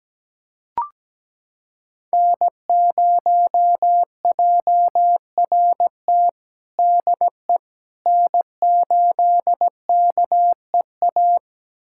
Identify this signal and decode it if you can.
Morse code, heard as a keyed tone, about 17 wpm, 700 Hz. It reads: N0JRT DE N8KEA
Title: USA Callsign DE Callsign - 5 Characters - 17wpm